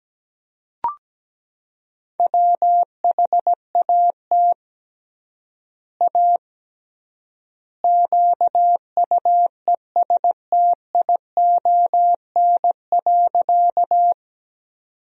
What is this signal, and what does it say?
Morse code, heard as a keyed tone, about 17 wpm, 700 Hz. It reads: WHAT A QUESTION.
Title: 3x Word-Spacing - Sentences from Top 400 Words - 17wpm